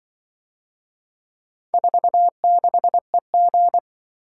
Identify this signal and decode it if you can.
Morse code, heard as a keyed tone, about 25 wpm, 700 Hz. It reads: 46EG
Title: Speed-Racing - Four Letter-Number Combinations - 20wpm